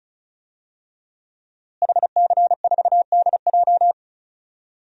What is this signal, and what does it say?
Morse code, heard as a keyed tone, about 34 wpm, 700 Hz. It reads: HC4DJ
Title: DXCC-100 Call Signs - 5 Characters - Repeated 3x - 35wpm